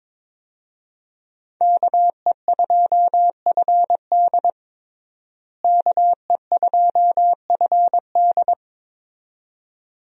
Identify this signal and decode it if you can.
Morse code, heard as a keyed tone, about 22 wpm, 700 Hz. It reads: KE2FD KE2FD
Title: USA Call Signs - 5 Characters - Repeated 2x - 22wpm